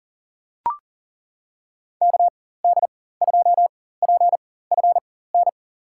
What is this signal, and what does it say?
Morse code, heard as a keyed tone, about 37 wpm, spaced as for 18 wpm, 700 Hz. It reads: KD2PFN
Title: USA Call Signs - 6 Characters - 4x Character Spacing - 40wpm